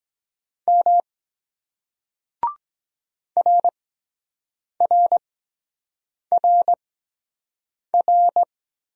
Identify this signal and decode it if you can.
Morse code, heard as a keyed tone, about 23 wpm, spaced as for 7 wpm, 700 Hz. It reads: M RRRR